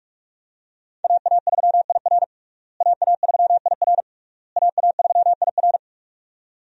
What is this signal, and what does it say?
Morse code, heard as a keyed tone, about 46 wpm, 700 Hz. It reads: AA3IR AA3IR AA3IR
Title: USA Call Signs - 5 Characters - Repeated 3x - 45wpm